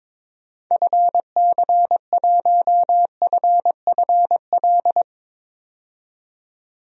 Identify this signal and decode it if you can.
Morse code, heard as a keyed tone, about 21 wpm, 700 Hz. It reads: FC1FFL
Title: DXCC-100 Call Signs - 6 Characters - 22wpm